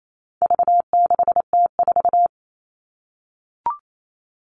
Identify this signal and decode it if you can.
Morse code, heard as a keyed tone, about 28 wpm, 700 Hz. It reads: V6T4
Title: All Cut Numbers - 4-Digit 28wpm